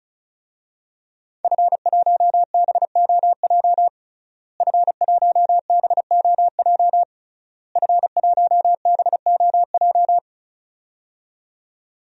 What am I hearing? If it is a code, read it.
Morse code, heard as a keyed tone, about 36 wpm, 700 Hz. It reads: F1BOJ F1BOJ F1BOJ
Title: DXCC-100 Call Signs - 5 Characters - Repeated 3x - 35wpm